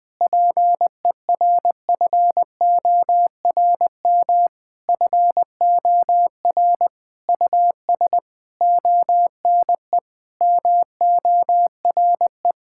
Morse code, heard as a keyed tone, about 20 wpm, 700 Hz. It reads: PERFORM FOR US ONE MORE
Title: Sentences from Top 1000 Words - Review Part 5 - 20wpm